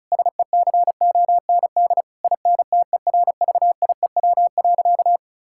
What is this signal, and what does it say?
Morse code, heard as a keyed tone, about 36 wpm, 700 Hz. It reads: SECOND INTERVIEW.
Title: Sentences from Top 1600 Words 35wpm